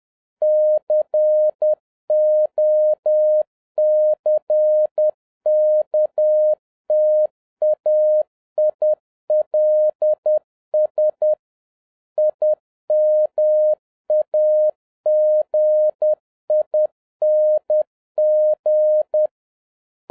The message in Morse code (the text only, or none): COCKTAILS IMAGING